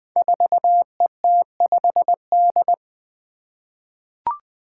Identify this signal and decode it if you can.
Morse code, heard as a keyed tone, about 20 wpm, 700 Hz. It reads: 4ET5D
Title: Five Challenging Letter-Number Combinations 20wpm